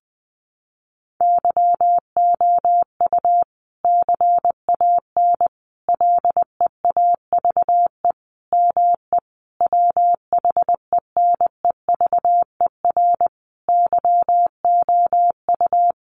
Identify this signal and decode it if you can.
Morse code, heard as a keyed tone, about 20 wpm, 700 Hz. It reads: YOU CAN LEAVE ME WHENEVER YOU
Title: Sentences from Top 1200 Words 20wpm